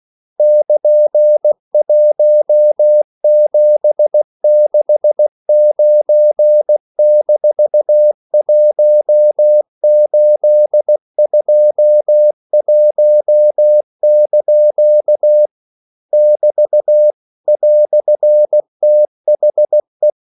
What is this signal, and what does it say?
Morse code, heard as a keyed tone, about 16 wpm, 600 Hz. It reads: <KN>1769-1821) <BT> "THE
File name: CW / Practice / EnglishQuotations0116